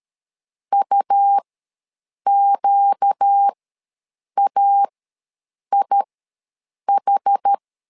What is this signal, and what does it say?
Morse code, heard as a keyed tone, about 13 wpm, spaced as for 7 wpm, 800 Hz. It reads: UQAIH
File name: LEZIONI / 4 Q1 ws 12wpm 2129